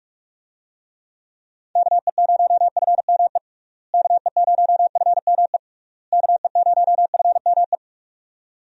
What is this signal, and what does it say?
Morse code, heard as a keyed tone, about 46 wpm, 700 Hz. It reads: KE0FME KE0FME KE0FME
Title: USA Call Signs - 6 Characters - Repeated 3x - 45wpm